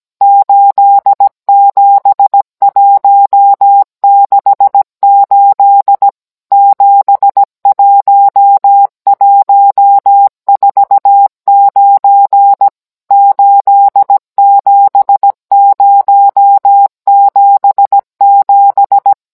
Morse code, heard as a keyed tone, about 17 wpm, 800 Hz. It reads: 87168 71149 87077